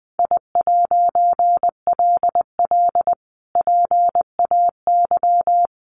Morse code, heard as a keyed tone, about 20 wpm, 700 Hz. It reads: I'LL PAY
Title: A Sound Of Thunder 20wpm